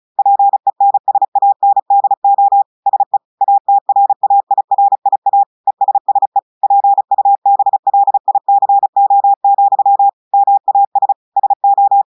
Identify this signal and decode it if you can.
Morse code, heard as a keyed tone, about 36 wpm, 800 Hz. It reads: PENSANDO SE ATRAIRIA ESSE PUBLICO, MAS SO
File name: audios / portugais / portugais texte en clair 2 vitesse 35 mots minutes